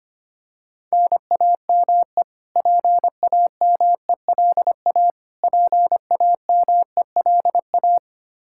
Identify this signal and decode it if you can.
Morse code, heard as a keyed tone, about 25 wpm, 700 Hz. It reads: NAME PAMELA PAMELA
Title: Practice QSO - Exchange 1 - Beginning - 25wpm